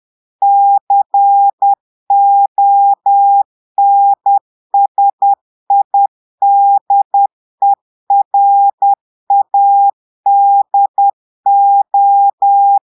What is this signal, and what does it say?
Morse code, heard as a keyed tone, about 10 wpm, 800 Hz. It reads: CONSIDERADO
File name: Portugais - texte en clair 1 - vitesse - 10 mots minute